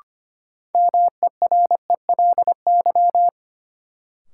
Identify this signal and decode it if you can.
Morse code, heard as a keyed tone, about 25 wpm, 700 Hz. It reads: MERELY
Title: Sentences from Top 1400 Words - New Word Drill Alt - 25wpm